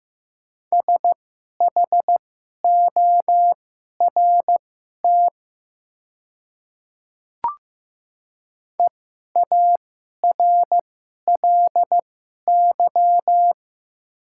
Morse code, heard as a keyed tone, about 15 wpm, spaced as for 11 wpm, 700 Hz. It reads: SHORT EARLY
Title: 2x Character Spacing - Top 500 Words - 15wpm